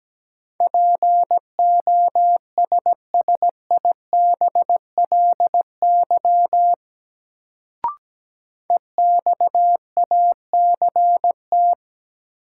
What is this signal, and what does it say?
Morse code, heard as a keyed tone, about 17 wpm, 700 Hz. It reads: POSSIBLY EXACT
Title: Sentences from Top 1300 Words - New Word Drill Alt - 17wpm